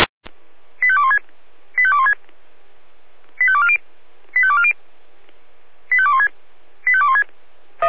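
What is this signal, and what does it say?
Five-tone selective call (ZVEI), heard as quick runs of five tones, tones of about 70 ms. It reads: 85216 85216 85240 85240 85216 85216
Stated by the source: ZVEI five-tone selective call 85244 received with an SDR.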